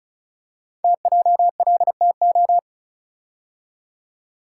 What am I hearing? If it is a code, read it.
Morse code, heard as a keyed tone, about 36 wpm, 700 Hz. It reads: TJLTO
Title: Code Groups - 5 Letters - 35wpm